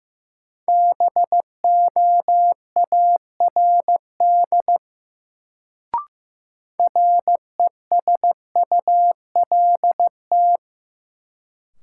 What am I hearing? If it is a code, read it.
Morse code, heard as a keyed tone, about 15 wpm, 700 Hz. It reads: BOARD RESULT
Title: Sentences from Top 700 Words - New Word Drill - 15wpm